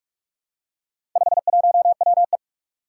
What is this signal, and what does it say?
Morse code, heard as a keyed tone, about 46 wpm, 700 Hz. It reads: 51WE
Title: Four Challenging Letter-Number Combinations 45wpm